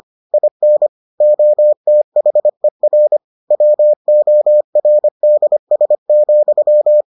Morse code, heard as a keyed tone, about 25 wpm, 600 Hz. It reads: IN OTHER WORDS,